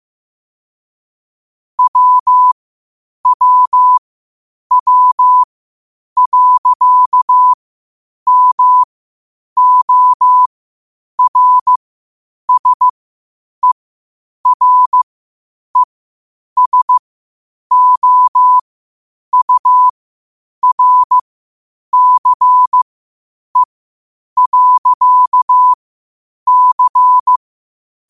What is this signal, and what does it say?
Morse code, heard as a keyed tone, about 15 wpm, spaced as for 8 wpm, 1000 Hz. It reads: WWW.MORSERESOURCE.C
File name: qotdfiles / 8wpm